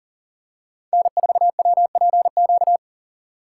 Morse code, heard as a keyed tone, about 40 wpm, 700 Hz. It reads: N4WPQ
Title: USA Call Signs - 5 Characters - 4x Character Spacing - 40wpm